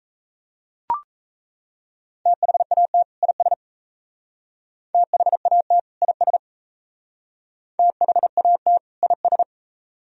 Morse code, heard as a keyed tone, about 35 wpm, 700 Hz. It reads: THAT IS THAT IS THAT IS